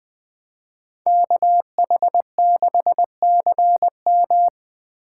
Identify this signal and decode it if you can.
Morse code, heard as a keyed tone, about 20 wpm, 700 Hz. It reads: KH6CM
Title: Speed Racing - USA 2x2 Call Signs 17wpm